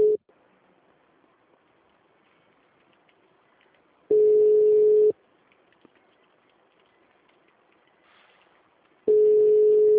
Ringback tone (Europe, 425 Hz).
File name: out- -2015- - - .14937